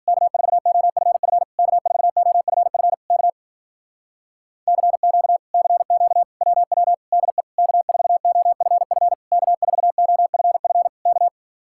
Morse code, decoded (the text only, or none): K4OFF K4OFF K CQ CQ WW DE K4OFF K4OFF K